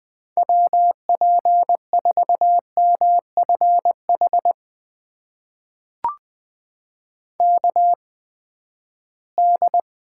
Morse code, heard as a keyed tone, about 20 wpm, spaced as for 14 wpm, 700 Hz. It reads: WP4MFH KD
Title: USA Call Signs - 8x Character Spacing - 20wpm